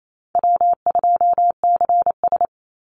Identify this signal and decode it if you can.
Morse code, heard as a keyed tone, about 28 wpm, 700 Hz. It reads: W2CS
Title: Easy Call Signs - 28wpm